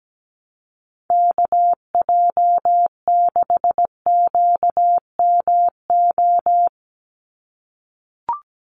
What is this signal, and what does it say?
Morse code, heard as a keyed tone, about 17 wpm, 700 Hz. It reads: KJ6QMO
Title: USA Call Signs 17wpm